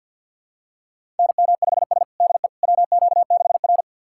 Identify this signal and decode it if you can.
Morse code, heard as a keyed tone, about 53 wpm, 690 Hz. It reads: NM5S DE WQ6R
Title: USA Callsign DE Callsign - 4 Characters - 50wpm